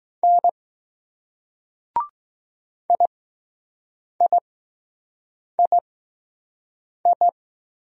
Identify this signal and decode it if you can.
Morse code, heard as a keyed tone, about 20 wpm, 700 Hz. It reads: N I I I I